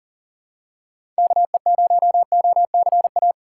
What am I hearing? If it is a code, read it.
Morse code, heard as a keyed tone, about 40 wpm, 700 Hz. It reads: KE0OCA
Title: USA Call Signs - 6 Characters - 4x Character Spacing - 40wpm